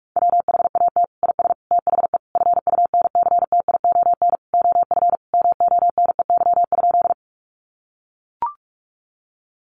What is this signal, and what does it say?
Morse code, heard as a keyed tone, about 44 wpm, 700 Hz. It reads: WHAT IS THE FUNCTION OF MONEY?
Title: Sentences from Top 1100 Words 45wpm